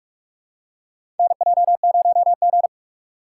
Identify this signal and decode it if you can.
Morse code, heard as a keyed tone, about 46 wpm, 700 Hz. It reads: NJ0G
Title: USA Call Signs - 4x Character Spacing - 45wpm